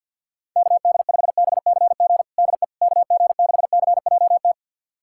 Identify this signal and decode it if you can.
Morse code, heard as a keyed tone, about 52 wpm, 700 Hz. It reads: KD5BCG DE KG6CJT
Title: USA Callsign DE Callsign 50wpm